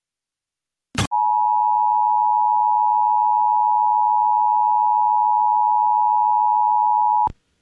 Emergency Alert System (SAME) tones (the 853 and 960 Hz attention tone). It is heard from 1.1 to 7.3 s.